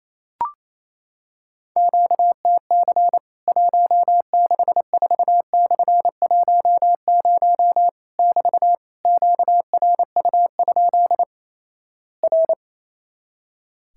Morse code, heard as a keyed tone, about 28 wpm, 700 Hz. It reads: QTC 164/10 <BT> QRU? R
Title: Worked All Europe DX Contest - QTC Exchange - 28wpm